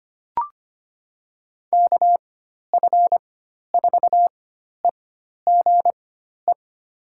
Morse code, heard as a keyed tone, about 25 wpm, spaced as for 12 wpm, 700 Hz. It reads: KF4EGE
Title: USA Call Signs - 6 Characters - 4x Character Spacing - 25wpm